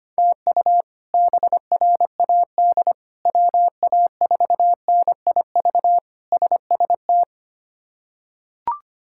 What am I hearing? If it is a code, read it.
Morse code, heard as a keyed tone, about 25 wpm, 700 Hz. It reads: TU BRAD WA4NIV SST
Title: K1USN SST QSO Practice - 25wpm